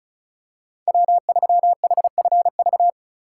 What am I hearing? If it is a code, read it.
Morse code, heard as a keyed tone, about 34 wpm, 700 Hz. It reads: W3HFV
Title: USA Call Signs - 5 Characters - 2x Character Spacing - 35wpm